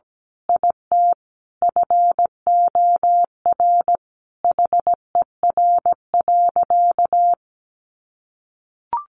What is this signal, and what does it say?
Morse code, heard as a keyed tone, about 17 wpm, 700 Hz. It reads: IT FOR HER.
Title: Sentences from Top 1400 Words 17wpm